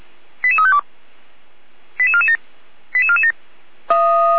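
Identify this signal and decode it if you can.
Five-tone selective call (ZVEI), heard as quick runs of five tones, tones of about 70 ms. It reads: 88352 88447 88447
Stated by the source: ZVEI five-tone selective call received on an SDR, logged by Feuerwehr Orlamünde (fire brigade).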